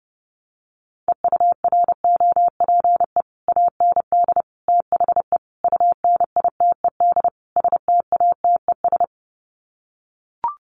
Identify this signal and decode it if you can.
Morse code, heard as a keyed tone, about 30 wpm, 700 Hz. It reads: EUROPE AND THE UNITED STATES
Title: Sets of 5 Words Encore2 30wpm